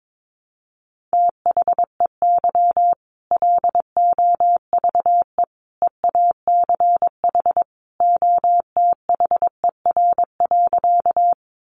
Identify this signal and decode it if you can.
Morse code, heard as a keyed tone, about 22 wpm, 700 Hz. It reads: THEY LOVE EACH OTHER.
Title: Easy Phrases 22wpm